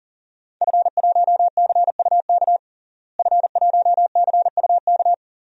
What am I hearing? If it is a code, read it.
Morse code, heard as a keyed tone, about 40 wpm, 700 Hz. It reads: F1CUK F1CUK
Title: DXCC-100 Call Signs - 5 Characters - Repeated 3x - 40wpm